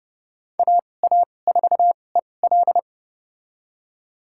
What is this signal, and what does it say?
Morse code, heard as a keyed tone, about 31 wpm, spaced as for 22 wpm, 690 Hz. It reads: AA4EL
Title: USA Call Signs - 5 Characters - 2x Character Spacing - 30wpm